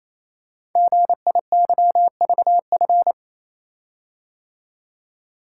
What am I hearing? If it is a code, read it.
Morse code, heard as a keyed tone, about 27 wpm, 700 Hz. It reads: GIYVF
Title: Code Groups - 5 Letters - 28wpm